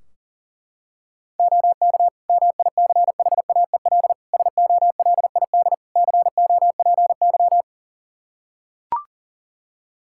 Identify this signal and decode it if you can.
Morse code, heard as a keyed tone, about 40 wpm, 700 Hz. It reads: OK MICHAEL SOLID COPY
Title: K1USN SST QSO Practice 40wpm